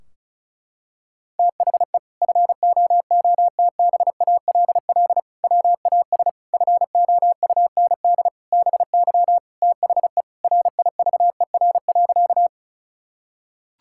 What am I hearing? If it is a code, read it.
Morse code, heard as a keyed tone, about 36 wpm, 700 Hz. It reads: THE FOOTBALL WAS FOUND BY THE RIVER.
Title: Sentences from Top 1600 Words 35wpm